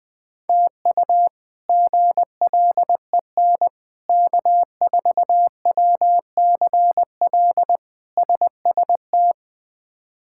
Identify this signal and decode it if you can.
Morse code, heard as a keyed tone, about 20 wpm, 700 Hz. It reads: TU GLEN K4WCL SST
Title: K1USN SST QSO Practice - 20wpm